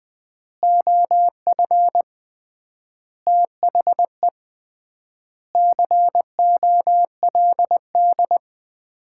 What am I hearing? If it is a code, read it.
Morse code, heard as a keyed tone, about 20 wpm, 700 Hz. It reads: OF THE COLD
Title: 3x Word-Spacing - Sets of 5 Words - 20wpm